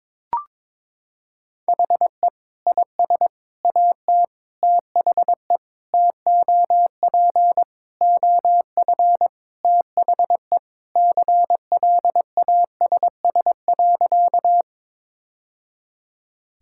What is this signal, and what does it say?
Morse code, heard as a keyed tone, about 22 wpm, 700 Hz. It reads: HE IS AT THE TOP OF THE CLASS.
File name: practice / 22wpm / Sentences with 3-Word Phrases - 22wpm